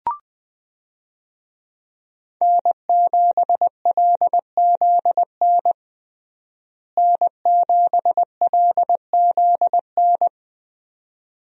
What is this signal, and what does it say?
Morse code, heard as a keyed tone, about 20 wpm, 700 Hz. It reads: N7LZN N7LZN
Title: USA Call Signs - 5 Characters - Repeated 2x - 20wpm